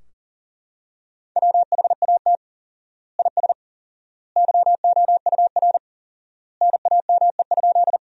Morse code, heard as a keyed tone, about 37 wpm, 700 Hz. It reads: WHAT IS YOUR NAME?
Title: 4x Word-Spacing - Sentences from Top 200 Words - 40wpm